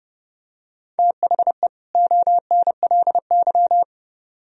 Morse code, heard as a keyed tone, about 28 wpm, 700 Hz. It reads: THE ONLY